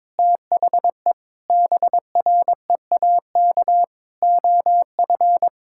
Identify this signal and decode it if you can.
Morse code, heard as a keyed tone, about 22 wpm, 700 Hz. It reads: THE BREAK OF
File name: Difficult Sentences with Hints - 22wpm